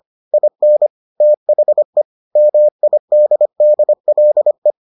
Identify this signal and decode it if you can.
Morse code, heard as a keyed tone, about 25 wpm, 600 Hz. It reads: IN THE MIDDLE